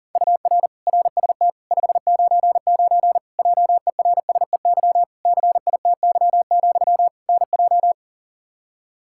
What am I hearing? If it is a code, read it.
Morse code, heard as a keyed tone, about 40 wpm, 700 Hz. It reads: UR RST 599 JERSEY CITY, NJ
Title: Practice QSO - Exchange 1 - Beginning - 40wpm